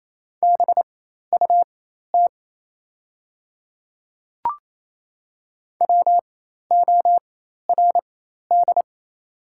Morse code, heard as a keyed tone, about 28 wpm, spaced as for 13 wpm, 690 Hz. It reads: BUT WORD
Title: 4x Character Spacing - Top 100 Words - 28wpm